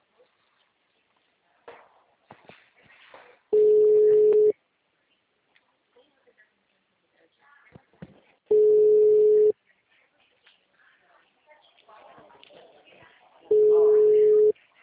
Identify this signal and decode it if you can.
Ringback tone (Europe, 425 Hz), with speech.